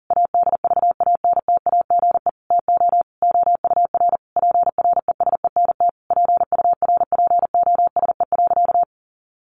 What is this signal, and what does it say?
Morse code, heard as a keyed tone, about 40 wpm, 700 Hz. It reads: ADVANTAGE TO OUR PRESENT PURPOSE.
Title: Sentences from Top 1400 Words 40wpm